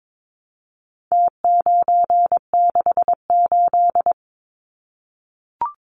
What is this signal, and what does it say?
Morse code, heard as a keyed tone, about 22 wpm, 700 Hz. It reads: T968
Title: Numbers - Contest Style - 4-Digit 22wpm